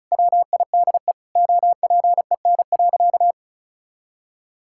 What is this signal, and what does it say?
Morse code, heard as a keyed tone, about 36 wpm, 700 Hz. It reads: WIDE OPEN.
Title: Sentences from Top 1000 Words - Review Part 6 - 35wpm